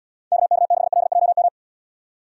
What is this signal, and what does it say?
Morse code, heard as a keyed tone, about 72 wpm, 700 Hz. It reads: KK6LJM